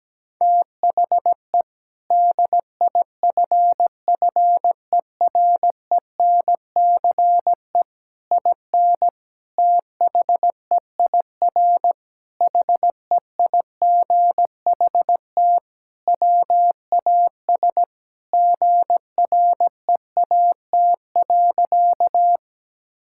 Morse code, heard as a keyed tone, about 17 wpm, 700 Hz. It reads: THE DIFFERENCE IN THEIR HEIGHT WAS GREAT.
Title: Sentences from Top 1600 Words 17wpm